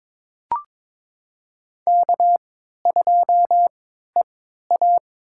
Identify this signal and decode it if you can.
Morse code, heard as a keyed tone, about 22 wpm, spaced as for 12 wpm, 700 Hz. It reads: K2EA
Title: USA Call Signs - 4 Characters - 3x Character Spacing - 22wpm